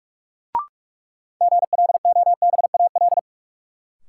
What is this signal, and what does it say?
Morse code, heard as a keyed tone, about 46 wpm, 700 Hz. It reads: GLOBAL
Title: Sentences from Top 1200 Words - New Word Drill Alt - 45wpm